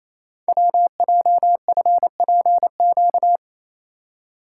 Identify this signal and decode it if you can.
Morse code, heard as a keyed tone, about 28 wpm, 700 Hz. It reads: WJFPQ
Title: Code Groups - 5 Letters - 28wpm